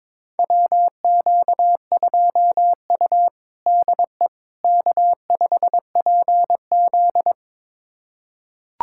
Morse code, heard as a keyed tone, about 22 wpm, 700 Hz. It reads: WQ2U DE K5PZ
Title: USA Callsign DE Callsign - 4 Characters - 22wpm